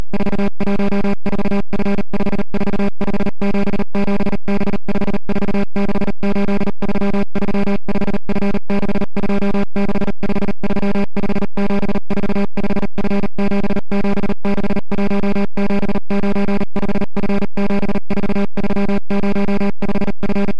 Morse code, heard as a keyed tone, about 38 wpm, 200 Hz. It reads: 414F54577B5468335F626535745F7761795F74305F